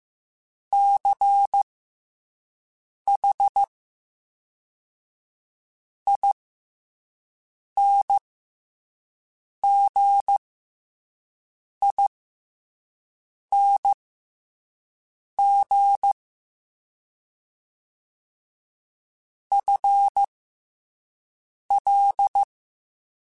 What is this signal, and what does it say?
Morse code, heard as a keyed tone, about 15 wpm, spaced as for 5 wpm, 800 Hz. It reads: CH INGING FL